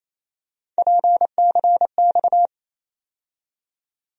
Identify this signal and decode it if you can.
Morse code, heard as a keyed tone, about 28 wpm, 700 Hz. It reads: PCX
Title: USA Call Sign Postfixes - 3 Letters 28wpm